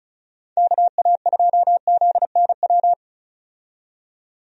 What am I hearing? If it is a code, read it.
Morse code, heard as a keyed tone, about 36 wpm, 700 Hz. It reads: KA2ZNW
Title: USA Call Signs - 6 Characters - 35wpm